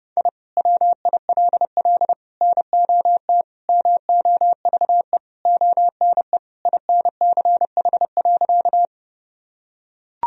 Morse code, heard as a keyed tone, about 30 wpm, 700 Hz. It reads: I WILL NOT MOVE ONE INCH.
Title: Sentences from Top 1000 Words - Review Part 3 - 30wpm